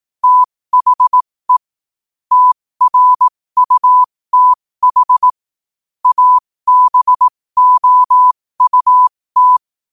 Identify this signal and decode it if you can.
Morse code, heard as a keyed tone, about 18 wpm, spaced as for 15 wpm, 1000 Hz. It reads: THE TRUTH ABOUT
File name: qotdfiles / 15wpm